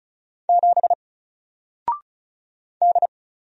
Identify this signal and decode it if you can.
Morse code, heard as a keyed tone, about 36 wpm, 700 Hz. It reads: 7 D
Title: Single Letter-Number Alt 35wpm